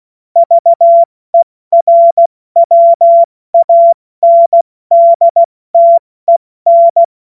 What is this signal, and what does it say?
Morse code, heard as a keyed tone, about 16 wpm, 680 Hz. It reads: VERWANDTEN
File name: Morsekurs / Witze / Witz 22. Juni 2021, 70 F80